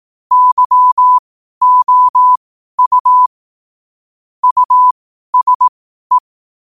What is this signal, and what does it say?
Morse code, heard as a keyed tone, about 18 wpm, spaced as for 12 wpm, 1000 Hz. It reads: YOU USE
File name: qotdfiles / 12wpm